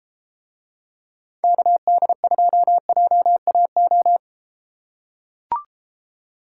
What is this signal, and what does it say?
Morse code, heard as a keyed tone, about 32 wpm, 700 Hz. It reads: KD2JAO